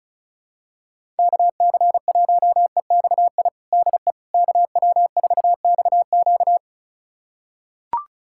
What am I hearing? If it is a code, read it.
Morse code, heard as a keyed tone, about 34 wpm, 700 Hz. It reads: KC1EXI DE KW4XQ
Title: USA Callsign DE Callsign 35wpm